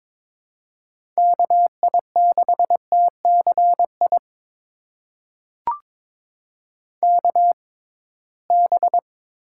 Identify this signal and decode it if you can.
Morse code, heard as a keyed tone, about 21 wpm, spaced as for 16 wpm, 700 Hz. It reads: KI6TCI KB
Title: USA Call Signs - 6x Character Spacing - 22wpm